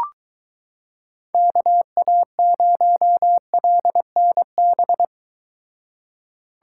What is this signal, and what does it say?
Morse code, heard as a keyed tone, about 22 wpm, 700 Hz. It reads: KA0LNB